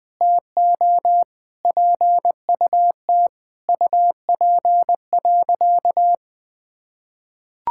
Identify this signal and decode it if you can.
Morse code, heard as a keyed tone, about 21 wpm, 700 Hz. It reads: TO PUT UP.
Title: Sentences from Top 1000 Words - Review Part 6 - 20wpm